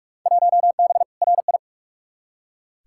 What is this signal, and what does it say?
Morse code, heard as a keyed tone, about 46 wpm, 700 Hz. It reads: 1B RI
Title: Field Day Exchanges Encore - 45wpm